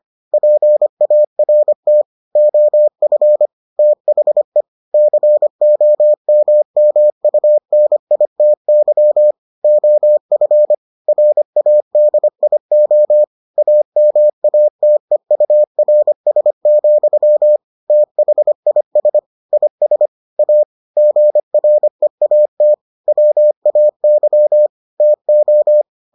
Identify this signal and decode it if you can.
Morse code, heard as a keyed tone, about 25 wpm, 600 Hz. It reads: PART OF THE COMMUNITY OF RADIO AMATEURS, THIS IS A GREAT WAY TO